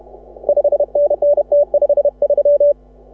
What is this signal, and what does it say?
Morse code, heard as a keyed tone, about 33 wpm, 530 Hz. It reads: 5NN T53
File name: AUDIO2018 / HA6OI CW